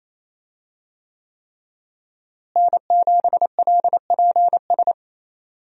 Morse code, heard as a keyed tone, about 28 wpm, 700 Hz. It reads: N7LPS
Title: USA Call Signs - 5 Characters - Repeated 3x - 28wpm